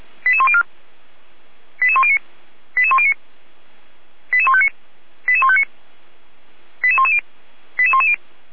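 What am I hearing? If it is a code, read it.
Five-tone selective call (ZVEI), heard as quick runs of five tones, tones of about 70 ms. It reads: 88104 88109 88109 88159 88159 88110 88110
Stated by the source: ZVEI five-tone selective call 88103 received by SDR on channel path 455, logged by the Feuerwehr Orlamünde audio server.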